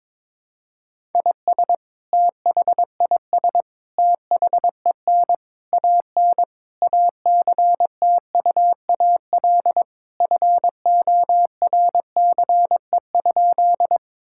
Morse code, heard as a keyed tone, about 22 wpm, 700 Hz. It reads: IS THIS THEN AN ACTUAL FORCE?
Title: Sentences from Top 1000 Words - Review Part 5 - 22wpm